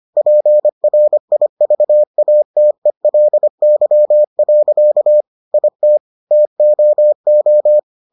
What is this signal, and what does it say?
Morse code, heard as a keyed tone, about 25 wpm, 600 Hz. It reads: PRIVATELY. IT TOO